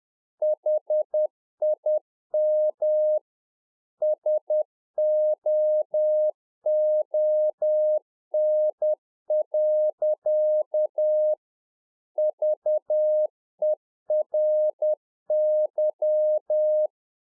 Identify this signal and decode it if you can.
Morse code, heard as a keyed tone, about 10 wpm, 610 Hz. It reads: HIM SOON. VERY
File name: UNFOGETTABLE QSO 10WPM